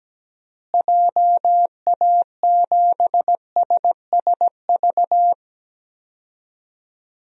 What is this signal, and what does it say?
Morse code, heard as a keyed tone, about 17 wpm, 700 Hz. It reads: JA7SSV
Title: DXCC-100 Call Signs - 6 Characters - Repeated 3x - 17wpm